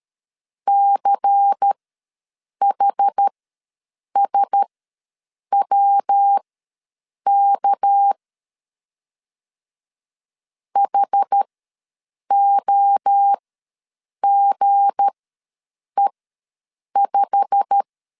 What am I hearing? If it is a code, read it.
Morse code, heard as a keyed tone, about 13 wpm, spaced as for 7 wpm, 800 Hz. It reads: CHSWK HOGE5
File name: LEZIONI / 10 SIGNS / 10A SLASH ws 12wpm 3443